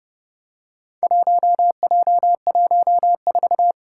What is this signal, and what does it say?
Morse code, heard as a keyed tone, about 30 wpm, 700 Hz. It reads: 1J14
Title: Four Challenging Letter-Number Combinations 30wpm